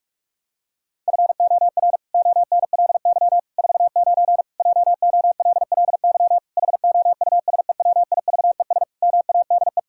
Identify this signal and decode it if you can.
Morse code, heard as a keyed tone, about 43 wpm, 700 Hz. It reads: FOR ONLY 49 JOLLY HOUSEWIVES MADE
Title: Pangram Challenge v2 - 45wpm